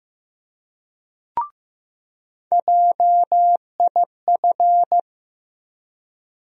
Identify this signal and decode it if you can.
Morse code, heard as a keyed tone, about 16 wpm, 700 Hz. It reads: JIF
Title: USA Call Sign Postfixes - 3 Letters 15wpm